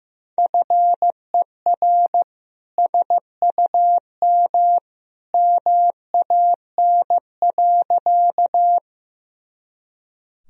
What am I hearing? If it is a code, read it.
Morse code, heard as a keyed tone, about 16 wpm, 700 Hz. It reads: FER SUM MAN.
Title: Sentences from Top 200 Words and Abbreviations 15wpm